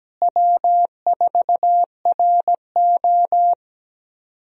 Morse code, heard as a keyed tone, about 17 wpm, 700 Hz. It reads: W4RO
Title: USA Call Signs - 4 Characters - Repeated 2x - 17wpm